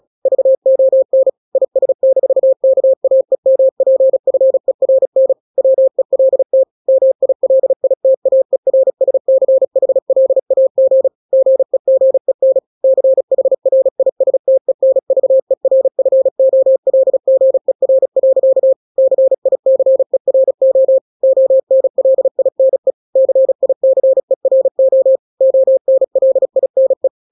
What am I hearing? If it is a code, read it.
Morse code, heard as a keyed tone, about 36 wpm, 530 Hz. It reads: VON IS-KAEMPFERN WELT MILITAERSCHLAG GEGEN CHRISTENVERFOLGER. CICERO ONLINE CICERO ONLINE
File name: daily / 35wpm